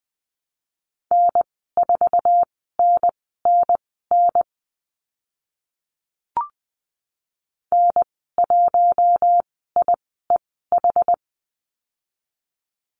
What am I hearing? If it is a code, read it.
Morse code, heard as a keyed tone, about 20 wpm, spaced as for 14 wpm, 700 Hz. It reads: N4NNN N1IEH
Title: 076 Lesson 13g - Letter L - Call Signs 20wpm